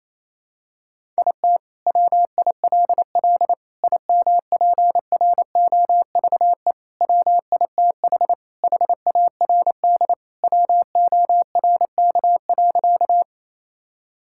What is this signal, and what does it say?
Morse code, heard as a keyed tone, about 28 wpm, 700 Hz. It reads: IT WILL IMPROVE WITH HARD WORK.
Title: Sentences from Top 1000 Words - Review Part 5 - 28wpm